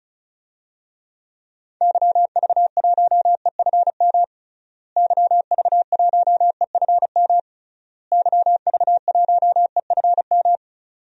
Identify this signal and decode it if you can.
Morse code, heard as a keyed tone, about 36 wpm, 700 Hz. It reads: YV1EFM YV1EFM YV1EFM
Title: DXCC-100 Call Signs - 6 Characters - Repeated 3x - 35wpm